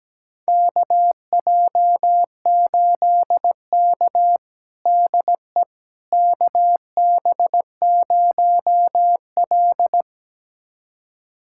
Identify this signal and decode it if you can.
Morse code, heard as a keyed tone, about 17 wpm, 700 Hz. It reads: KJ8K DE KB0L
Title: USA Callsign DE Callsign - 4 Characters - 17wpm